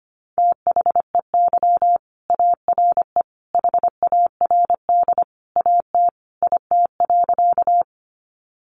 Morse code, heard as a keyed tone, about 25 wpm, 700 Hz. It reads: THEY ARE HARD AT IT.
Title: Sentences from Top 300 Words 25wpm